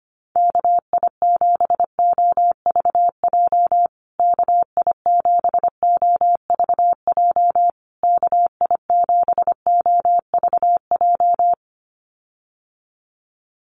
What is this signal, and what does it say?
Morse code, heard as a keyed tone, about 25 wpm, 700 Hz. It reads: KI7OVJ KI7OVJ KI7OVJ
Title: USA Call Signs Repeated 3x 25wpm